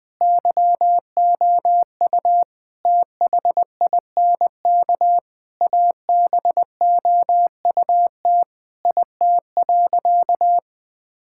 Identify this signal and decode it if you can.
Morse code, heard as a keyed tone, about 20 wpm, 700 Hz. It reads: YOU THINK ABOUT IT.
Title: Sentences with 7-Word Phrases - 20wpm